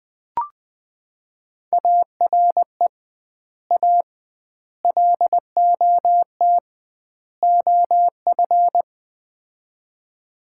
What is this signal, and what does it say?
Morse code, heard as a keyed tone, about 20 wpm, 700 Hz. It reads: ARE A LOT OF
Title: 2x Word-Spacing - Sets of 4 Words - 20wpm